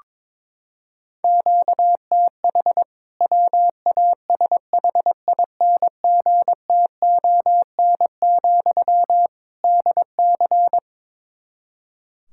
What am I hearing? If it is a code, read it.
Morse code, heard as a keyed tone, about 22 wpm, 700 Hz. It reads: QTH WASHINGTON, DC
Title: K1USN SST QSO Practice 22wpm